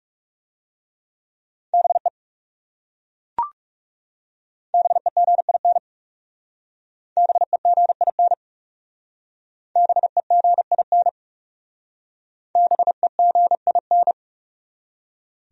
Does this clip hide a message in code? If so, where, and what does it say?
Yes: Morse code from 1.7 to 14.1 s, heard as a keyed tone, about 39 wpm, 700 Hz. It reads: BE BEGIN BEGIN BEGIN BEGIN